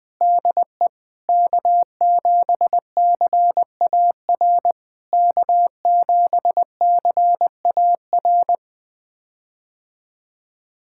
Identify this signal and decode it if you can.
Morse code, heard as a keyed tone, about 20 wpm, 700 Hz. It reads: DE K7CAR K7CAR
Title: POTA QSO Practice - 20wpm